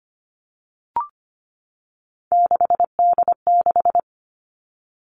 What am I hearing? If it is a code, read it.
Morse code, heard as a keyed tone, about 25 wpm, 700 Hz. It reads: 6D6
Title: All Cut Numbers - 3-Digit 25wpm